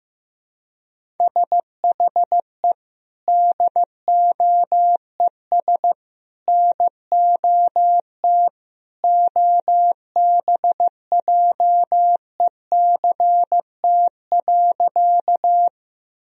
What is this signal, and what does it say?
Morse code, heard as a keyed tone, about 15 wpm, 700 Hz. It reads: SHE DOES NOT OBJECT.
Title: Sentences from Top 1000 Words - Review Part 3 - 15wpm